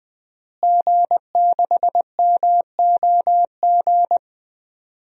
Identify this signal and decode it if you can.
Morse code, heard as a keyed tone, about 20 wpm, 700 Hz. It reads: G6MOG
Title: DXCC-100 Call Signs - 5 Characters - Repeated 3x - 20wpm